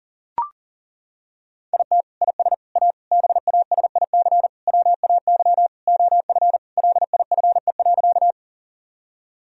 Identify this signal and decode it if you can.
Morse code, heard as a keyed tone, about 40 wpm, 700 Hz. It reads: IT IS A BASIC WAY OF LIFE.
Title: Sentences from Top 1000 Words - Review Part 3 - 40wpm